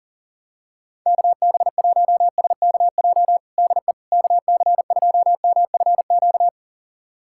Morse code, heard as a keyed tone, about 40 wpm, 700 Hz. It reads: KB1SKJ DE KC2MFQ
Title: USA Callsign DE Callsign 40wpm